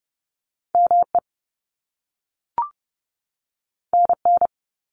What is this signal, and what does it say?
Morse code, heard as a keyed tone, about 30 wpm, 700 Hz. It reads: ME NN
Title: Top Two Letter Combinations in Words 30wpm